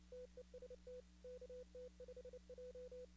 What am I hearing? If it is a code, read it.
Morse code, heard as a keyed tone, about 29 wpm, 500 Hz. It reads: TEST KT5J
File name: contests / 2019-cw-wpx-kt5j-k5pi- -2330 0-44